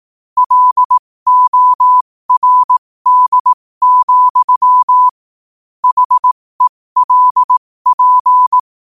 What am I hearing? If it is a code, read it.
Morse code, heard as a keyed tone, about 18 wpm, spaced as for 15 wpm, 1000 Hz. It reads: LORD, HELP
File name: qotdfiles / 15wpm